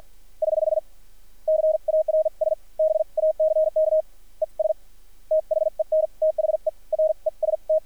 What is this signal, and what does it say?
Morse code, heard as a keyed tone, about 46 wpm, spaced as for 39 wpm, 590 Hz. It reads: <HH> CARS BACK ES THEN THE REST
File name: archived / CRSnet